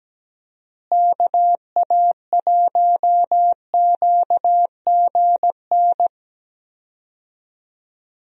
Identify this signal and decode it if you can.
Morse code, heard as a keyed tone, about 17 wpm, 700 Hz. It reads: KA1QGN
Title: USA Call Signs - 6 Characters - Repeated 3x - 17wpm